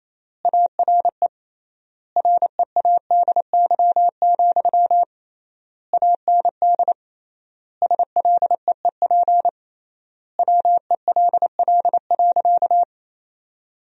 Morse code, heard as a keyed tone, about 28 wpm, 700 Hz. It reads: ARE READY, AND SLEEP WELL.
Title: 3x Word-Spacing - Sentences from Top 500 Words - 28wpm